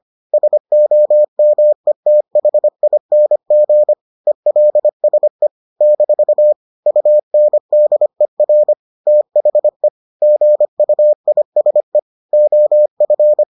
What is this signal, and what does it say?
Morse code, heard as a keyed tone, about 25 wpm, 600 Hz. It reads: SOMETHING ELSE - UNDER THE GUISE OF